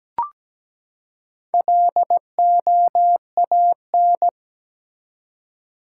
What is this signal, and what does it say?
Morse code, heard as a keyed tone, about 17 wpm, 700 Hz. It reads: LOAN
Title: Four Letter Words - 17wpm